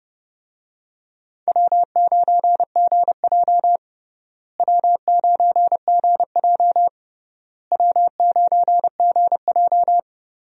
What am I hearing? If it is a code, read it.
Morse code, heard as a keyed tone, about 30 wpm, 700 Hz. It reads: W9GJ W9GJ W9GJ
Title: USA Call Signs - 4 Characters - Repeated 3x - 30wpm